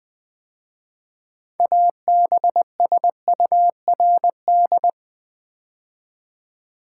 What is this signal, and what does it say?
Morse code, heard as a keyed tone, about 21 wpm, 700 Hz. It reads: ABSURD
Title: 6 Letter Words - First 3 Letters Given - 20wpm